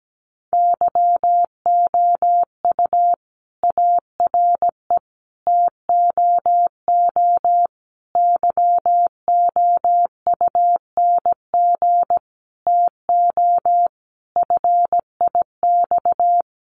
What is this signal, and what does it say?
Morse code, heard as a keyed tone, about 17 wpm, 700 Hz. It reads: YOU ARE TOO YOUNG TO FIX
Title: Sentences from Top 1400 Words 17wpm